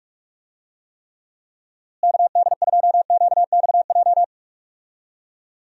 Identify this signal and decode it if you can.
Morse code, heard as a keyed tone, about 46 wpm, 700 Hz. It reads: KD2QXJ
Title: USA Call Signs - 6 Characters - Repeated 2x - 45wpm